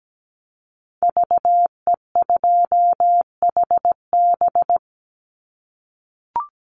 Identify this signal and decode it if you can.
Morse code, heard as a keyed tone, about 17 wpm, 700 Hz. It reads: VE2HB
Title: Call Signs - 17wpm